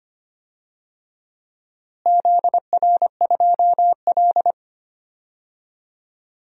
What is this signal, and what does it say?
Morse code, heard as a keyed tone, about 25 wpm, 700 Hz. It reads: ZR2L
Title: DXCC-100 Call Signs - 4 Characters - Repeated 3x - 25wpm